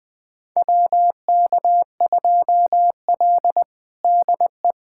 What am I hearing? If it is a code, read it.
Morse code, heard as a keyed tone, about 20 wpm, 700 Hz. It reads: WK2L DE
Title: USA Callsign DE Callsign - 4 Characters - 20wpm